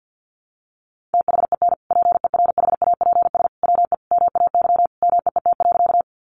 Morse code, heard as a keyed tone, about 50 wpm, 700 Hz. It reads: THEN PERHAPS WE MAY MEET.
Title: Sentences from Top 1100 Words 50wpm